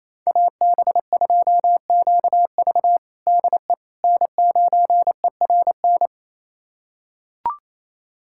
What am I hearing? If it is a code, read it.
Morse code, heard as a keyed tone, about 28 wpm, 700 Hz. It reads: AB2QV DE N9ERN
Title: USA Callsign DE Callsign - 5 Characters - 28wpm